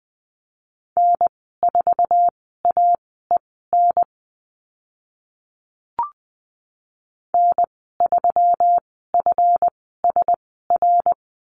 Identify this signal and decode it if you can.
Morse code, heard as a keyed tone, about 20 wpm, spaced as for 14 wpm, 700 Hz. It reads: N4AEN N3FSR
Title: 140 Lesson 23g - Letter f - Call Signs 20wpm